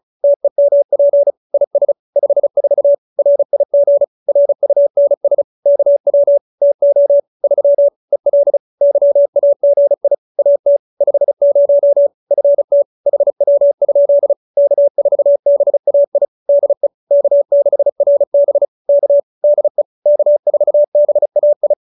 Morse code, heard as a keyed tone, about 34 wpm, 570 Hz. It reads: TEMP IS 54 RIG RUNS KW TO 3 EL YAGI AT 50 FT HW? K4BAI DE K6RB K DE K4BAI